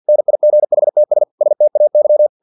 Morse code, heard as a keyed tone, about 50 wpm, 600 Hz. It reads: NIGHTS STAY
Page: Never 49wpm